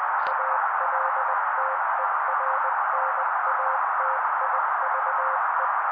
Morse code, heard as a keyed tone, about 20 wpm, 530 Hz. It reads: ALTERNATIVE
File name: daily / 18wpm noise